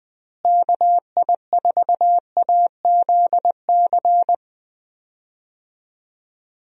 Morse code, heard as a keyed tone, about 20 wpm, 700 Hz. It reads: KI4AZC
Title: USA Call Signs - 6 Characters - Repeated 3x - 20wpm